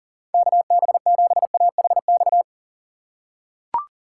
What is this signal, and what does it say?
Morse code, heard as a keyed tone, about 40 wpm, 700 Hz. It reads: KB7AHX
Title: USA Call Signs - 6 Characters - 3x Character Spacing - 40wpm